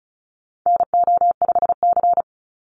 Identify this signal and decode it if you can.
Morse code, heard as a keyed tone, about 34 wpm, 700 Hz. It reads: NO5C
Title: USA Call Signs 35wpm